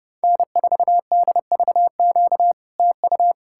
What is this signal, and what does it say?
Morse code, heard as a keyed tone, about 40 wpm, 700 Hz. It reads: N4DVQ TU